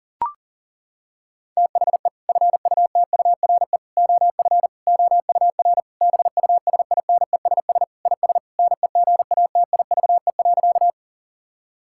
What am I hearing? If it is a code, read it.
Morse code, heard as a keyed tone, about 40 wpm, 700 Hz. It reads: THE FUTURE OF OUR BUSINESS IS NEGATIVE.
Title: Sentences from Top 1000 Words - Review Part 6 - 40wpm